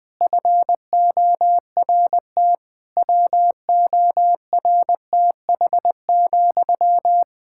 Morse code, heard as a keyed tone, about 20 wpm, 700 Hz. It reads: FORT WORTH,
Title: Practice QSO - Exchange 1 - Beginning - 20wpm